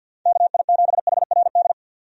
Morse code, heard as a keyed tone, about 50 wpm, 700 Hz. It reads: KI7HRD
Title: USA Call Signs - 6 Characters - 50wpm